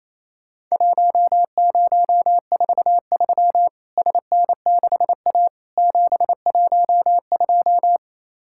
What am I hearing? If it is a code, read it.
Morse code, heard as a keyed tone, about 28 wpm, 700 Hz. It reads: 1043 SN6A 712
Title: Worked All Europe DX Contest - QTC Exchange - 28wpm